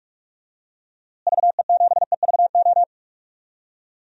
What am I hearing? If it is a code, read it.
Morse code, heard as a keyed tone, about 46 wpm, 700 Hz. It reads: VE7EVO
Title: DXCC-100 Call Signs - 6 Characters - Repeated 2x - 45wpm